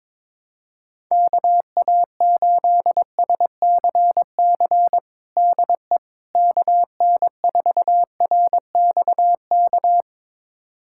Morse code, heard as a keyed tone, about 22 wpm, 700 Hz. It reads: KA8SCC DE KN4RXK
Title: USA Callsign DE Callsign 22wpm